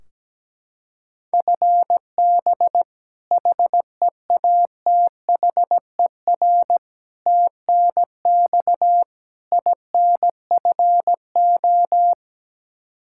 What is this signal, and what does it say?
Morse code, heard as a keyed tone, about 17 wpm, 700 Hz. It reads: FB HEATHER TNX INFO
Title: Practice QSO - Exchange 3 - More Info - 17wpm